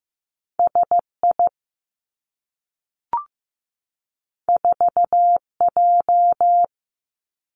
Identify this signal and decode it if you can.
Morse code, heard as a keyed tone, about 15 wpm, 700 Hz. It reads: S I 4 J
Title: Two Challenging Letter-Number Combinations 15wpm